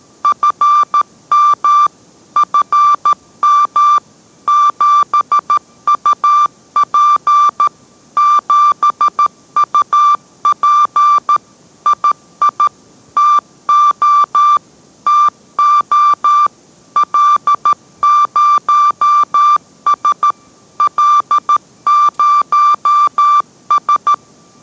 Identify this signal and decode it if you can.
Morse code, heard as a keyed tone, about 13 wpm, 1180 Hz. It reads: FM FM 7UP 7UP II TO TO L0S L0S